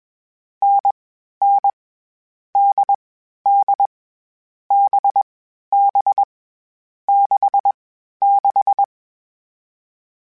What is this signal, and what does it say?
Morse code, heard as a keyed tone, about 20 wpm, 800 Hz. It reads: N N D D B B 6 6
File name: Difficult Combos / Difficult Combos-mp3s / NDB NDB6 @20